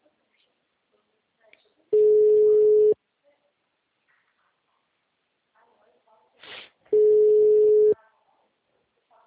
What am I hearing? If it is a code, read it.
Ringback tone (Europe, 425 Hz), with speech.
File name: out- -2001- - - .16886